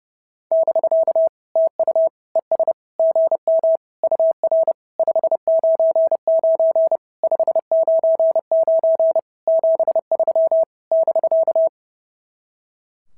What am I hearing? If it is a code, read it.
Morse code, heard as a keyed tone, about 28 wpm, 650 Hz. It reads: <BK> TU ES GM UR 599 599 73 <BK>
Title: SOTA QSO Practice 30wpm